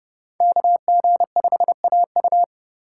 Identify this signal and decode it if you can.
Morse code, heard as a keyed tone, about 30 wpm, 700 Hz. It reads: KG5AU
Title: USA Call Signs - 5 Characters - 3x Character Spacing - 30wpm